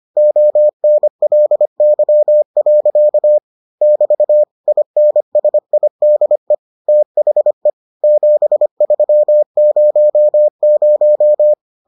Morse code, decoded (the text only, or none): ONLY. <BT> INSIDE THE 7300